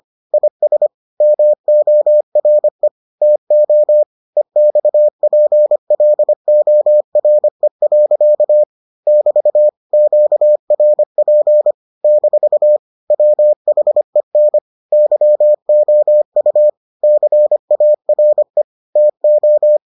Morse code, heard as a keyed tone, about 25 wpm, 600 Hz. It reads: IS MORE TO EXPLORE. <BT> QRP - WHEN YOU CARE TO